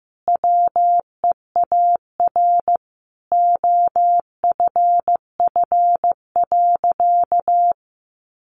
Morse code, heard as a keyed tone, about 15 wpm, 700 Hz. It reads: WEAR OFF.
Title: Sentences from Top 1500 Words 15wpm